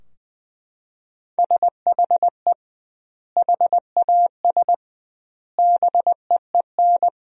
Morse code, heard as a keyed tone, about 20 wpm, 700 Hz. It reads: SHE HAS BEEN
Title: 2x Word-Spacing - Sentences from Top 200 Words - 20wpm